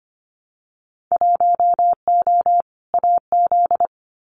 Morse code, heard as a keyed tone, about 25 wpm, 700 Hz. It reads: 1O AZ
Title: Winter Field Day Exchanges 25wpm